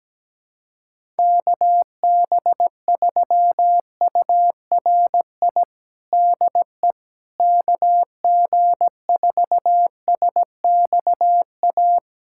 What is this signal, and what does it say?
Morse code, heard as a keyed tone, about 17 wpm, 700 Hz. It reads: KB3URI DE KG4SXA
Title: USA Callsign DE Callsign 17wpm